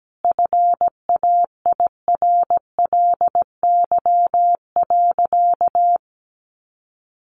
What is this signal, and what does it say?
Morse code, heard as a keyed tone, about 17 wpm, 700 Hz. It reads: FAIRLY.
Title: Sentences from Top 1100 Words 17wpm